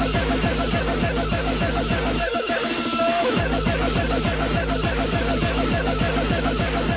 Two-tone teleprinter signal, about 200 Hz shift.